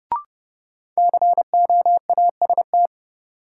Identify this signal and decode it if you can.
Morse code, heard as a keyed tone, about 32 wpm, 700 Hz. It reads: COAST
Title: Sentences from Top 1000 Words - Review Part 4 - New Word Drill - 30wpm